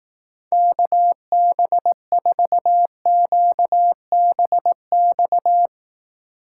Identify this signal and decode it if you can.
Morse code, heard as a keyed tone, about 18 wpm, 700 Hz. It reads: KB4QBX
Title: Speed Racing - USA 2x3 Call Signs 15wpm